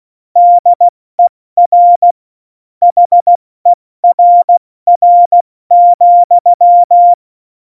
Morse code, heard as a keyed tone, about 16 wpm, 700 Hz. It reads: DER HERR,
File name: Morsekurs / Witze / Witz 24. November 2020, 70 F80